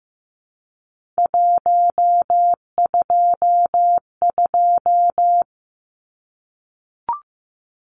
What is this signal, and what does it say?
Morse code, heard as a keyed tone, about 15 wpm, 700 Hz. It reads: 122
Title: Numbers - 3-Digit 15wpm